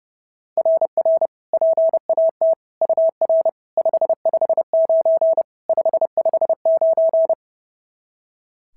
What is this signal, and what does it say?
Morse code, heard as a keyed tone, about 30 wpm, 690 Hz. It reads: RR PAT UR 559 559
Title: SOTA QSO Practice 30wpm